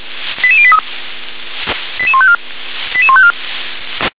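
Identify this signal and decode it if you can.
Five-tone selective call (ZVEI), heard as quick runs of five tones, tones of about 70 ms. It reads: 88083 88154 88154
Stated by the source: ZVEI five-tone selective call received on an SDR.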